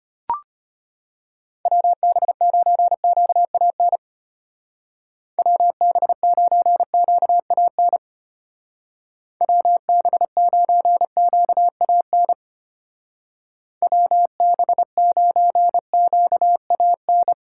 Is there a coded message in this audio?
Yes: Morse code from 1.7 to 17.3 s, heard as a keyed tone, about 33 wpm, 700 Hz. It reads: WB9QAN WB9QAN WB9QAN WB9QAN